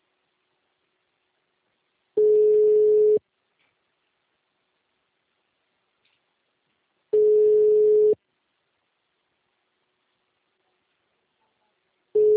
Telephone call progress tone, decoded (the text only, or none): ringback tone